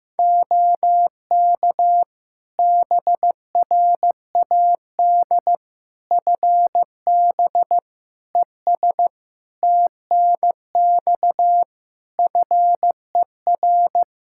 Morse code, heard as a keyed tone, about 15 wpm, 700 Hz. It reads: OK BRAD FB ES TNX FER
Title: K1USN SST QSO Practice 15wpm